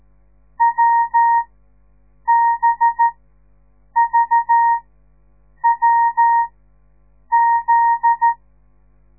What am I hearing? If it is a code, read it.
Morse code, heard as a keyed tone, about 13 wpm, spaced as for 7 wpm, 930 Hz. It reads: WBVWZ
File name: radiomorningstar / morse / les 9